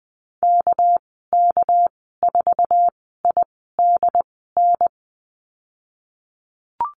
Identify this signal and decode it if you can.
Morse code, heard as a keyed tone, about 20 wpm, spaced as for 14 wpm, 700 Hz. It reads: KK4IDN
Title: 219 Lesson 35g - Number 0 - Call Signs 20wpm